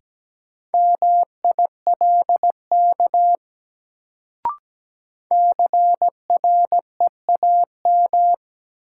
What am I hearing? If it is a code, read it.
Morse code, heard as a keyed tone, about 17 wpm, 700 Hz. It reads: MILK CREAM
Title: Sentences from Top 1500 Words - New Word Drill Alt - 17wpm